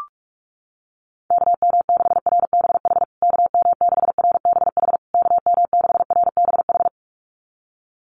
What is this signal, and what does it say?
Morse code, heard as a keyed tone, about 44 wpm, 700 Hz. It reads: KM6RBH KM6RBH KM6RBH
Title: USA Call Signs Repeated 3x 45wpm